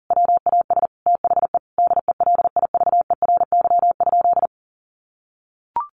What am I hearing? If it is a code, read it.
Morse code, heard as a keyed tone, about 40 wpm, 700 Hz. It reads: WAS THE DELIVERY?
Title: Sentences from Top 1500 Words 40wpm